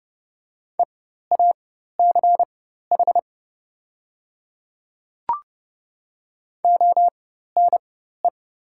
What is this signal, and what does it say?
Morse code, heard as a keyed tone, about 30 wpm, spaced as for 14 wpm, 700 Hz. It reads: EACH ONE
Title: 4x Character Spacing - Top 300 Words - 30wpm